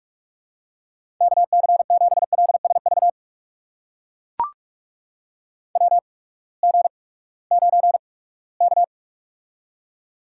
Morse code, heard as a keyed tone, about 44 wpm, spaced as for 22 wpm, 700 Hz. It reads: KC7LSV WG9K
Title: USA Call Signs - 8x Character Spacing - 45wpm